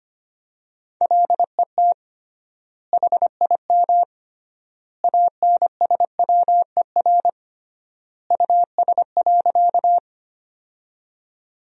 Morse code, heard as a keyed tone, about 25 wpm, 700 Hz. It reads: LET HIM ANSWER US.
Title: 3x Word-Spacing - Sentences from Top 300 Words - 25wpm